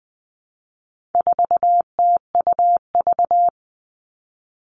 Morse code, heard as a keyed tone, about 20 wpm, 700 Hz. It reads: 4TUV
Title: All Cut Numbers - 4-Digit 20wpm